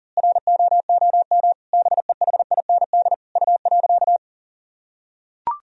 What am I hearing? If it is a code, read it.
Morse code, heard as a keyed tone, about 40 wpm, 700 Hz. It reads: ROOM BEHIND U.
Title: Sentences from Top 500 Words and Abbreviations 40wpm